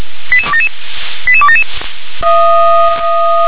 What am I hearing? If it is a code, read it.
Five-tone selective call (ZVEI), heard as quick runs of five tones, tones of about 80 ms. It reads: 88277 88277